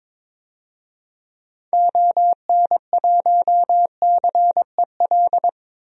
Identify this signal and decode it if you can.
Morse code, heard as a keyed tone, about 22 wpm, 700 Hz. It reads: ON1CEL
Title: DXCC-100 Call Signs - 6 Characters - 22wpm